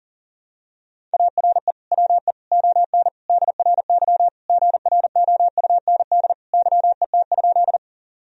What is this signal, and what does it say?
Morse code, heard as a keyed tone, about 40 wpm, 700 Hz. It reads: ARE WE ON DRY GROUND YET?
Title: Sentences from Top 1000 Words - Review Part 3 - 40wpm